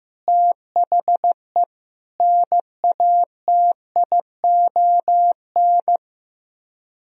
Morse code, heard as a keyed tone, about 15 wpm, 700 Hz. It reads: THE NATION
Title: Speed-Racing - Sets of 5 Words 15wpm